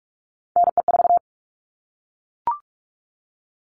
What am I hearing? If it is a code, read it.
Morse code, heard as a keyed tone, about 44 wpm, 700 Hz. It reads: NE4
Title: All Cut Numbers - 3-Digit 45wpm